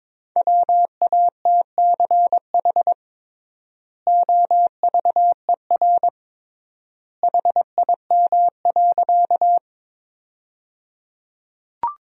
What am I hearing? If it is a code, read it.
Morse code, heard as a keyed tone, about 22 wpm, 700 Hz. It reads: WATCH OVER HIM.
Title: 3x Word-Spacing - Sentences from Top 300 Words - 22wpm